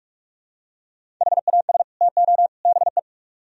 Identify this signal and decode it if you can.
Morse code, heard as a keyed tone, about 44 wpm, 700 Hz. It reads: HAS TO BE